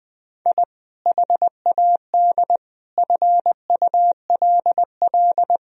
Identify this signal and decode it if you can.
Morse code, heard as a keyed tone, about 21 wpm, 700 Hz. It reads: I HAD FULL
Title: Sentences from Top 1000 Words - Review Part 3 - 20wpm